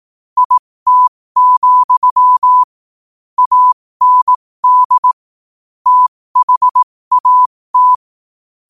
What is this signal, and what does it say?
Morse code, heard as a keyed tone, about 18 wpm, spaced as for 15 wpm, 1000 Hz. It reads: IT, AND THAT
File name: qotdfiles / 15wpm